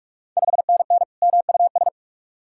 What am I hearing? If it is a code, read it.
Morse code, heard as a keyed tone, about 44 wpm, 690 Hz. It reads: 5NN MUS
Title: Ontario QSO Party Exchange 45wpm